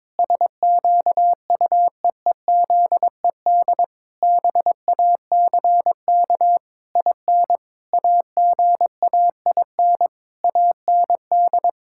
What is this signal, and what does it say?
Morse code, heard as a keyed tone, about 22 wpm, 700 Hz. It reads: SQUEEZED BACK IN AGAIN AND
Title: Pangram Challenge v2 - 22wpm